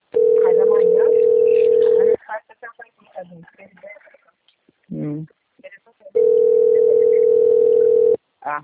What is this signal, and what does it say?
Ringback tone (North America), 440 + 480 Hz, with speech.
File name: rg-1600-unknown- - - .45367